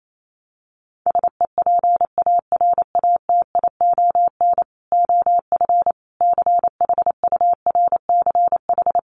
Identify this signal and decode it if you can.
Morse code, heard as a keyed tone, about 28 wpm, 700 Hz. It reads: SEPARATION OF CHURCH
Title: Sets of 5 Words 28wpm